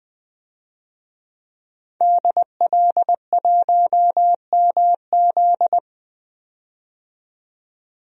Morse code, heard as a keyed tone, about 21 wpm, 700 Hz. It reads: DL1MZ
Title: DXCC-100 Call Signs - 5 Characters - 20wpm